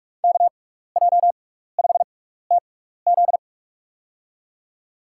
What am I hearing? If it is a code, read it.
Morse code, heard as a keyed tone, about 46 wpm, spaced as for 16 wpm, 690 Hz. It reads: KJ5TZ
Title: USA Call Signs - 5 Characters - 6x Character Spacing - 45wpm